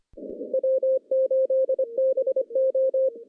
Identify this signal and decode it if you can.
Morse code, heard as a keyed tone, about 25 wpm, 530 Hz. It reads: W8BO
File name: contests / CW-SS-k5tr- -0400